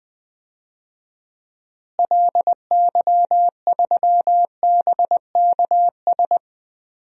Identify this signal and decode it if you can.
Morse code, heard as a keyed tone, about 20 wpm, 700 Hz. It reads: LY3BKS
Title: DXCC-100 Call Signs - 6 Characters - Repeated 3x - 20wpm